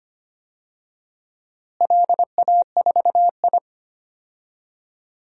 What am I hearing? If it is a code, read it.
Morse code, heard as a keyed tone, about 25 wpm, 700 Hz. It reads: LA4I
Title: DXCC-100 Call Signs - 4 Characters - Repeated 3x - 25wpm